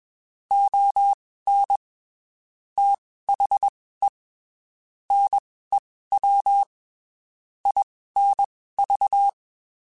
Morse code, heard as a keyed tone, about 21 wpm, spaced as for 15 wpm, 800 Hz. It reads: ON THE NEW INV
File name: (spaced) / QSOs Spaced-mp3s / WD0FBT 1 @20 sp